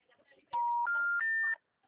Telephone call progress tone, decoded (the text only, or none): special information tone